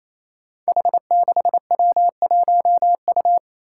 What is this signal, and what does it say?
Morse code, heard as a keyed tone, about 27 wpm, 700 Hz. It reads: H6W1U
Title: Five Challenging Letter-Number Combinations 28wpm